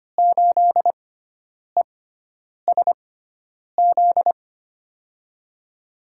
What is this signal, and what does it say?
Morse code, heard as a keyed tone, about 25 wpm, spaced as for 9 wpm, 700 Hz. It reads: 8ESZ
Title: USA Call Signs - 6x Character Spacing - 25wpm